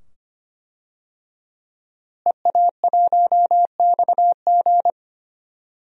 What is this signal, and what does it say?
Morse code, heard as a keyed tone, about 25 wpm, 700 Hz. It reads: EA1XG
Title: DXCC-100 Call Signs - 5 Characters - Repeated 2x - 25wpm